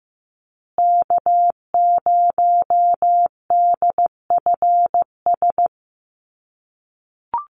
Morse code, heard as a keyed tone, about 15 wpm, 700 Hz. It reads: K0DFS
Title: USA Call Signs Repeated 3x 15wpm